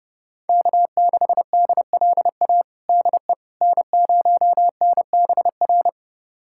Morse code, heard as a keyed tone, about 30 wpm, 700 Hz. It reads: K6DLA DE N0NBR
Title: USA Callsign DE Callsign - 5 Characters - 30wpm